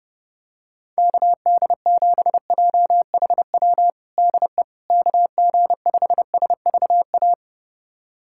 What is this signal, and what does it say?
Morse code, heard as a keyed tone, about 32 wpm, 700 Hz. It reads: KD7JHW DE KG5SVA
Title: USA Callsign DE Callsign - 6 Characters - 30wpm